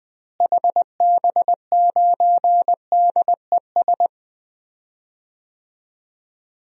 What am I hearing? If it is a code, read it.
Morse code, heard as a keyed tone, about 21 wpm, 700 Hz. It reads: HB9DES
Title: DXCC-100 Call Signs - 6 Characters - Repeated 2x - 20wpm